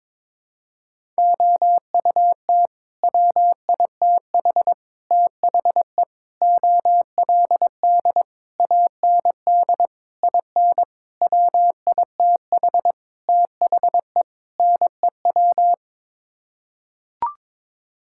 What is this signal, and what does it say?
Morse code, heard as a keyed tone, about 22 wpm, 700 Hz. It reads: OUT WITH THE OLD AND IN WITH THE NEW
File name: Sets of 9 Words - 22wpm